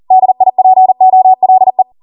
Morse code, heard as a keyed tone, about 40 wpm, 750 Hz. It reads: DIPOLE
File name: Morse / 40WPM